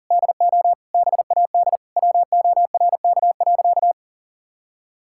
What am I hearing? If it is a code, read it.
Morse code, heard as a keyed tone, about 40 wpm, 700 Hz. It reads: DO BAD WORK.
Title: Sentences from Top 1000 Words - Review Part 6 - 40wpm